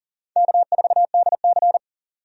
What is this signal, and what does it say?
Morse code, heard as a keyed tone, about 40 wpm, 700 Hz. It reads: K4DC
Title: USA Call Signs - 4 Characters - 40wpm